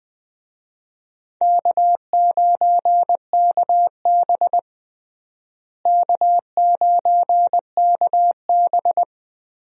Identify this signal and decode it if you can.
Morse code, heard as a keyed tone, about 20 wpm, 700 Hz. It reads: K9KB K9KB
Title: USA Call Signs - 4 Characters - Repeated 3x - 20wpm